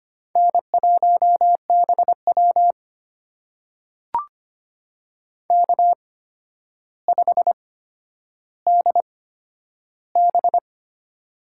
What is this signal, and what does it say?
Morse code, heard as a keyed tone, about 25 wpm, spaced as for 11 wpm, 700 Hz. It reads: N1BW K5DB
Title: USA Call Signs - 4 Characters - 8x Character Spacing - 25wpm